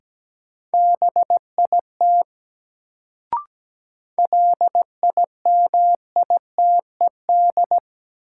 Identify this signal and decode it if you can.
Morse code, heard as a keyed tone, about 17 wpm, 700 Hz. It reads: BIT LIMITED
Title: Sentences from Top 900 Words - New Word Drill - 17wpm